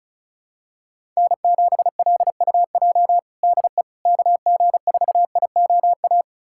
Morse code, heard as a keyed tone, about 36 wpm, 700 Hz. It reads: N7LUJ DE KG4IOA
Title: USA Callsign DE Callsign 35wpm